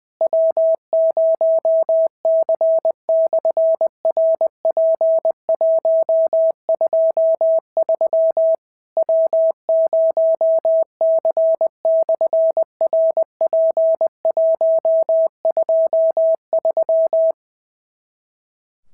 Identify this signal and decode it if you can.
Morse code, heard as a keyed tone, about 20 wpm, 650 Hz. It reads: W0C/RP123 W0C/RP123
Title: SOTA QSO Practice 20wpm